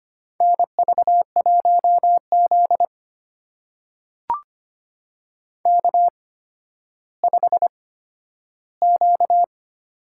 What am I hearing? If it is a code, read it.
Morse code, heard as a keyed tone, about 25 wpm, spaced as for 12 wpm, 700 Hz. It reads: NV1Z K5Q
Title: USA Call Signs - 8x Character Spacing - 25wpm